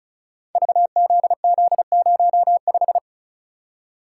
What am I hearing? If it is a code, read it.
Morse code, heard as a keyed tone, about 36 wpm, 700 Hz. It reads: VZZ05
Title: Code Groups - 5 Letter:Numbers - 35wpm